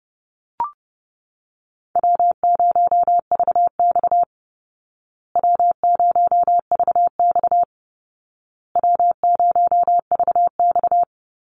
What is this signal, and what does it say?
Morse code, heard as a keyed tone, about 30 wpm, 700 Hz. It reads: W0VX W0VX W0VX
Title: Call Signs Repeated 2x - 30wpm